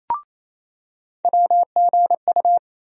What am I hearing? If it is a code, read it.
Morse code, heard as a keyed tone, about 28 wpm, 700 Hz. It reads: WGU
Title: USA Call Sign Postfixes - 3 Letters 28wpm